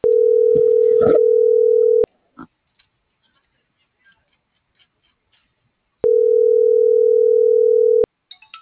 Ringback tone (North America), 440 + 480 Hz.